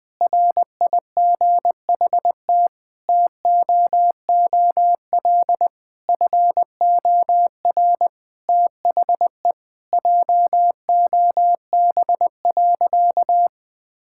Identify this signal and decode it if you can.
Morse code, heard as a keyed tone, about 20 wpm, 700 Hz. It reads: RIGHT TOOL FOR THE JOB.
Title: Sentences from Top 1000 Words - Review Part 6 - 20wpm